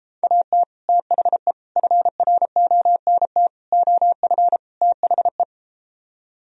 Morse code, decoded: AT THE FRONT OF THE

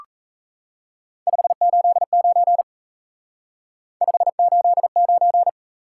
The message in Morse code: 589 589

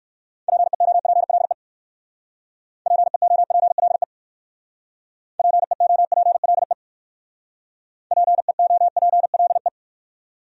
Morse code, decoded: PEOPLE PEOPLE PEOPLE PEOPLE

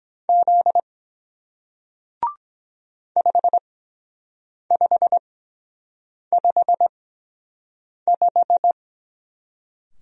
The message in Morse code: Z 5555

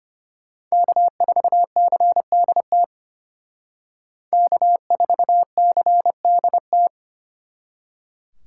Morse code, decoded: K4CDT K4CDT

